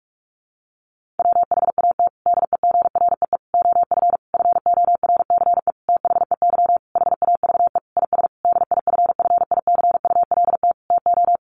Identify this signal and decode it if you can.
Morse code, heard as a keyed tone, about 44 wpm, 700 Hz. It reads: WHAT DEGREE OF FORCE THEY HAVE IS DIFFICULT TO
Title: Sentences from Top 1500 Words 45wpm